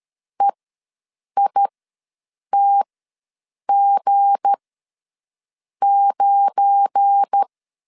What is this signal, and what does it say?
Morse code, heard as a keyed tone, about 13 wpm, spaced as for 7 wpm, 800 Hz. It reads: EITG9
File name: LEZIONI / 7 GF9Z / 7 GF9 ws 12wpm 3344